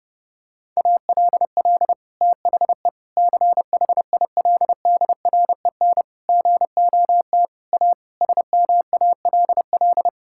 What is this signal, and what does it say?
Morse code, heard as a keyed tone, about 30 wpm, 700 Hz. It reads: ALL THE CHILDREN GOT A SMALL
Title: Sentences from Top 1000 Words - Review Part 6 - 30wpm